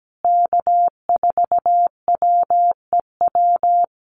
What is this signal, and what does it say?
Morse code, heard as a keyed tone, about 17 wpm, 700 Hz. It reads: K4WEW
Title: USA Call Signs Repeated 2x 17wpm